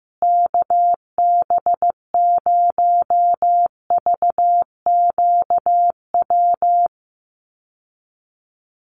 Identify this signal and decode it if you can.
Morse code, heard as a keyed tone, about 15 wpm, 700 Hz. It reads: KB0VQW
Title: USA Call Signs Repeated 2x 15wpm